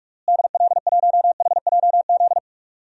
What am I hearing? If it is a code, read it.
Morse code, heard as a keyed tone, about 46 wpm, 700 Hz. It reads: DL1HJZ
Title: DXCC-100 Call Signs - 6 Characters - Repeated 3x - 45wpm